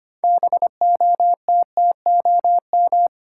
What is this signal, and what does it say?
Morse code, heard as a keyed tone, about 25 wpm, 700 Hz. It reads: BOTTOM
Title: Sets of 3 Words - New Word Drill - 25wpm